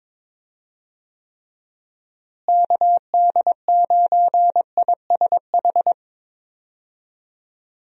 Morse code, heard as a keyed tone, about 22 wpm, 700 Hz. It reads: KD9ISH
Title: USA Call Signs - 6 Characters - Repeated 3x - 22wpm